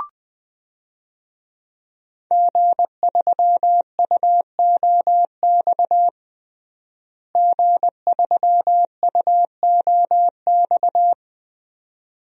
Morse code, heard as a keyed tone, about 20 wpm, 700 Hz. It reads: G3UOX G3UOX
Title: DXCC-100 Call Signs - 5 Characters - Repeated 2x - 20wpm